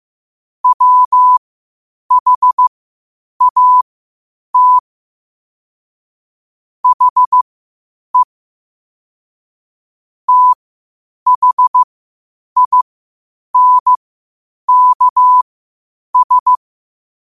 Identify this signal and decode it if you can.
Morse code, heard as a keyed tone, about 15 wpm, spaced as for 8 wpm, 1000 Hz. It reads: WHAT HE THINKS
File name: qotdfiles / 8wpm